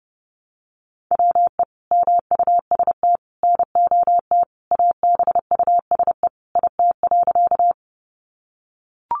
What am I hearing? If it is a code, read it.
Morse code, heard as a keyed tone, about 30 wpm, 700 Hz. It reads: WE MUST NOT ABUSE IT.
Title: Sentences from Top 1300 Words 30wpm